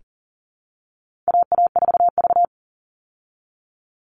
Morse code, heard as a keyed tone, about 40 wpm, 700 Hz. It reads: AA4V
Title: Call Signs Repeated 2x - 40wpm